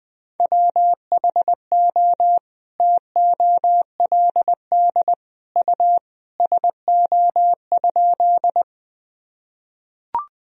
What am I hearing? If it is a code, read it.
Morse code, heard as a keyed tone, about 21 wpm, 700 Hz. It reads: WHO TOLD U SO?
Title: Sentences from Top 400 Words and Abbreviations 20wpm